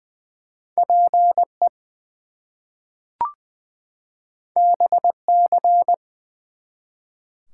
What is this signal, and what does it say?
Morse code, heard as a keyed tone, about 20 wpm, 700 Hz. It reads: PE BC
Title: US States and Canadian Provinces 20wpm